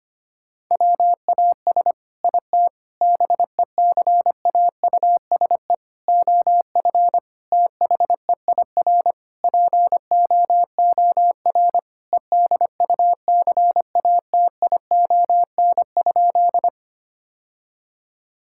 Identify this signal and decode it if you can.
Morse code, heard as a keyed tone, about 25 wpm, 700 Hz. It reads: WAS IT BECAUSE OF THEIR POOR EDUCATION?
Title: Sentences from Top 1000 Words - Review Part 6 - 25wpm